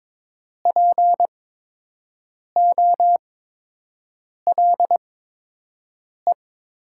Morse code, heard as a keyed tone, about 22 wpm, spaced as for 6 wpm, 700 Hz. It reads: POLE
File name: Top 90 QSO Elements - 8x Character Spacing - 22wpm